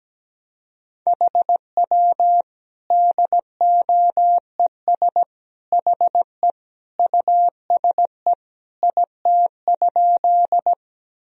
Morse code, heard as a keyed tone, about 17 wpm, 700 Hz. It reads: HW DOES HE USE IT?
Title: Sentences from Top 200 Words and Abbreviations 17wpm